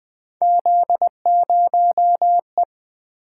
Morse code, heard as a keyed tone, about 20 wpm, 700 Hz. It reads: Z0E
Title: Three Letter-Number Combinations - 20wpm